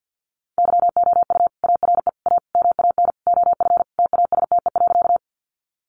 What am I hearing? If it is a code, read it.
Morse code, heard as a keyed tone, about 50 wpm, 700 Hz. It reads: YOU ARE A MAN OF TASTE.
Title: Sentences from Top 1100 Words 50wpm